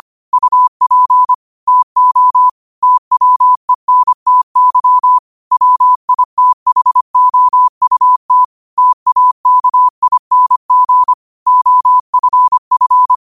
Morse code, heard as a keyed tone, about 25 wpm, 1000 Hz. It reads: UP TO TWENTY WITHOUT TAKING OFF
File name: qotdfiles / 25wpm